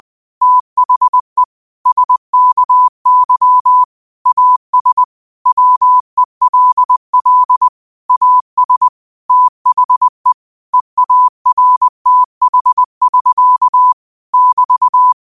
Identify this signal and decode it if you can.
Morse code, heard as a keyed tone, about 20 wpm, 1000 Hz. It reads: THE SKY AS WELL AS THE EARTH<SK> <BT>
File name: qotdfiles / 20wpm